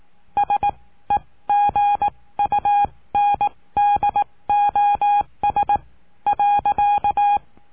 Morse code, heard as a keyed tone, about 18 wpm, spaced as for 15 wpm, 820 Hz. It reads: SEGUNDOS.
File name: cursos / CURSO DE TELEGRAFIA / Telegrafia - Trilha 20